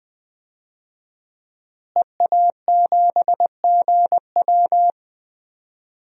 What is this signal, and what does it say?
Morse code, heard as a keyed tone, about 20 wpm, 700 Hz. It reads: EA7GW
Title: DXCC-100 Call Signs - 5 Characters - Repeated 2x - 20wpm